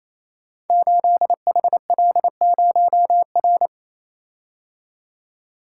Morse code, heard as a keyed tone, about 27 wpm, 700 Hz. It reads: 8HL0R
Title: Code Groups - 5 Letter:Numbers - 28wpm